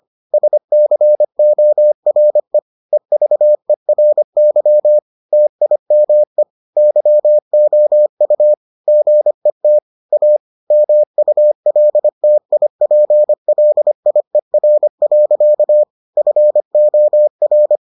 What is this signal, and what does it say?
Morse code, heard as a keyed tone, about 25 wpm, 600 Hz. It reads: SCORE EVERY TIME YOU GET A MULTIPLIER. FOR